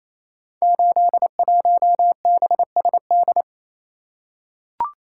Morse code, heard as a keyed tone, about 29 wpm, 700 Hz. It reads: 81BSD
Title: Code Groups - 5 Letter:Numbers - 28wpm